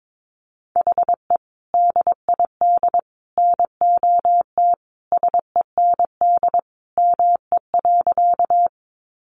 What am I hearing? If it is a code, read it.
Morse code, heard as a keyed tone, about 22 wpm, 700 Hz. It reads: HE DID NOT SEND ME.
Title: Sentences from Top 1500 Words 22wpm